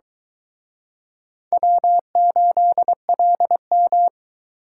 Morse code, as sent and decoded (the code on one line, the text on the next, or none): .-- ---.. .-.. --
W8LM